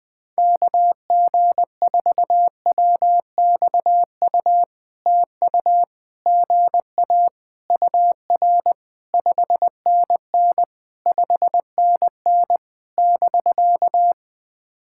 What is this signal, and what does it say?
Morse code, heard as a keyed tone, about 20 wpm, 700 Hz. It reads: KG4WXU TU GA UR 5NN 5NN <BK>
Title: POTA QSO Practice - 20wpm